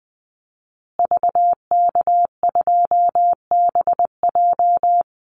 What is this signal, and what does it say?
Morse code, heard as a keyed tone, about 19 wpm, 700 Hz. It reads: VK2BJ
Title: Call Signs Repeated 2x - 20wpm